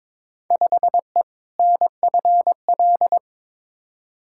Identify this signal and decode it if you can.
Morse code, heard as a keyed tone, about 22 wpm, 700 Hz. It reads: 5E NFL
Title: Field Day Exchanges Encore - 22wpm